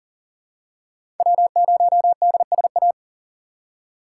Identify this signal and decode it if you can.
Morse code, heard as a keyed tone, about 40 wpm, 700 Hz. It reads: W0DSA
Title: USA Call Signs - 5 Characters - 3x Character Spacing - 40wpm